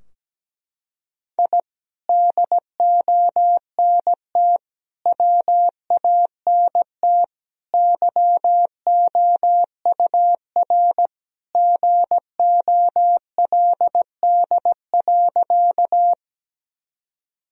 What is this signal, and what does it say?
Morse code, heard as a keyed tone, about 17 wpm, 700 Hz. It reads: I DONT WANT YOUR GOLD.
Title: Sentences from Top 1000 Words - Review Part 3 - 17wpm